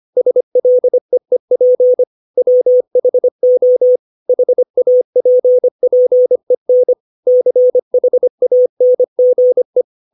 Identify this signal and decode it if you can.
Morse code, heard as a keyed tone, about 25 wpm, 500 Hz. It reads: SLEEP WHO HAPPEN CHANGE
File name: 632 words / 632 Words 25WPM